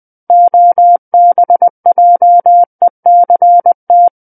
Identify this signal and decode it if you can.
Morse code, heard as a keyed tone, about 20 wpm, 700 Hz. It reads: OBJECT
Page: SKCC Learning Center - Morse Code Practice Files